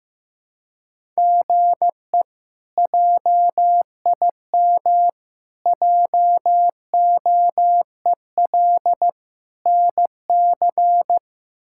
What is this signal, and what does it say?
Morse code, heard as a keyed tone, about 15 wpm, 700 Hz. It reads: GE JIM JOEL NC
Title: K1USN SST QSO Practice - 15wpm